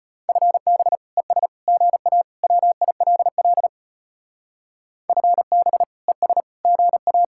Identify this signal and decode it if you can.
Morse code, heard as a keyed tone, about 29 wpm, 700 Hz. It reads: FB ES GA WILL FB ES GA
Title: Speed-Racing Common QSO Phrases Encore - 25wpm